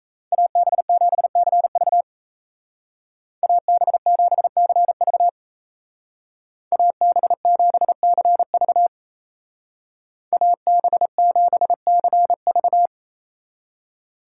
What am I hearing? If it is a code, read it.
Morse code, heard as a keyed tone, about 37 wpm, 700 Hz. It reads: AB7CV AB7CV AB7CV AB7CV